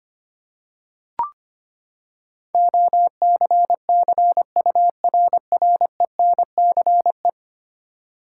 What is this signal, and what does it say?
Morse code, heard as a keyed tone, about 25 wpm, 700 Hz. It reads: OCCURRENCE
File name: Ten Letter Words - 25wpm